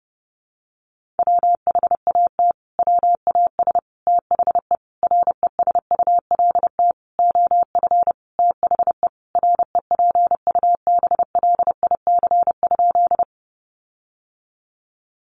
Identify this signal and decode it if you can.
Morse code, heard as a keyed tone, about 28 wpm, 700 Hz. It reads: WHAT WAS THE RESULT OF THE REPUBLIC?
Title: Sentences from Top 1500 Words 30wpm